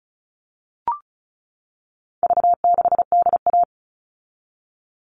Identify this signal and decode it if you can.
Morse code, heard as a keyed tone, about 34 wpm, 700 Hz. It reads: V6DA
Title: All Cut Numbers - 4-Digit 35wpm